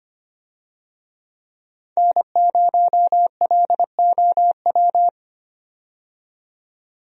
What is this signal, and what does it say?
Morse code, heard as a keyed tone, about 25 wpm, 700 Hz. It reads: N0LOW
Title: USA Call Signs - 5 Characters - 25wpm